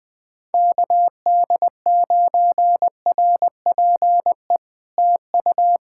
Morse code, heard as a keyed tone, about 20 wpm, 700 Hz. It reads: KD9RPE TU
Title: POTA QSO Practice - 20wpm